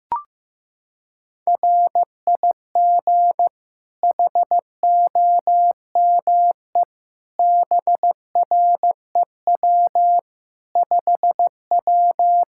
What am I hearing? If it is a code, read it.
Morse code, heard as a keyed tone, about 15 wpm, 700 Hz. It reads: RIG HOME BREW 5W
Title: Practice QSO - Exchange 2 - Info - 15wpm